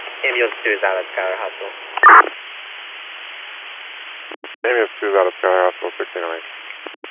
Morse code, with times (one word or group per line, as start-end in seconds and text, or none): none